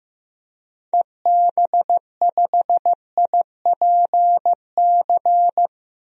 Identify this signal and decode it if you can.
Morse code, heard as a keyed tone, about 15 wpm, 700 Hz. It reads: EB5IPC
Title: DXCC-100 Call Signs - 6 Characters - 15wpm